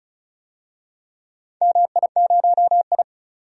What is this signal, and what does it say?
Morse code, heard as a keyed tone, about 36 wpm, 700 Hz. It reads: MI0I
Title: DXCC-100 Call Signs - 4 Characters - 35wpm